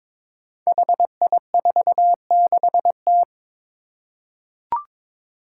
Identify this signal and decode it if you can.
Morse code, heard as a keyed tone, about 22 wpm, 700 Hz. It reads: HI46T
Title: Five Challenging Letter-Number Combinations 22wpm